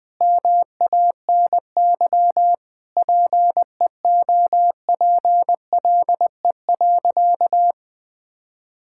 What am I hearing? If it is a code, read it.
Morse code, heard as a keyed tone, about 20 wpm, 700 Hz. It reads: MANY PEOPLE.
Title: Difficult Sentences with Hints - 20wpm